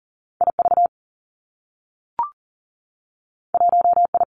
Morse code, heard as a keyed tone, about 41 wpm, 700 Hz. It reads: I V 1 I
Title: Two Challenging Letter-Number Combinations 40wpm